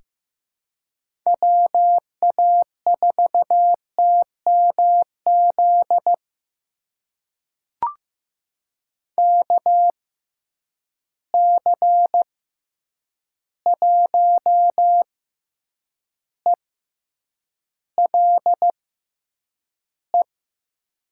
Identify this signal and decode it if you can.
Morse code, heard as a keyed tone, about 15 wpm, spaced as for 8 wpm, 700 Hz. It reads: WA4TMZ KC1ELE
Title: USA Call Signs - 6 Characters - 6x Character Spacing - 15wpm